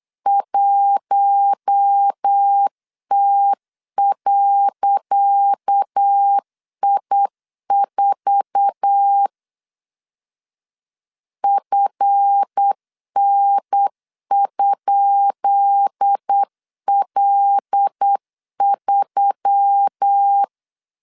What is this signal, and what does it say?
Morse code, heard as a keyed tone, about 8 wpm, 800 Hz. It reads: 1T.I4 FN?L3
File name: LEZIONI / 10 SIGNS / 10C DOT 8wpm 3649